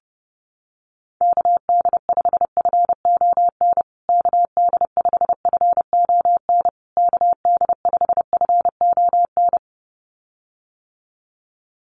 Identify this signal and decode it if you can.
Morse code, heard as a keyed tone, about 30 wpm, 700 Hz. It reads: KD5FON KD5FON KD5FON
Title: USA Call Signs Repeated 3x 30wpm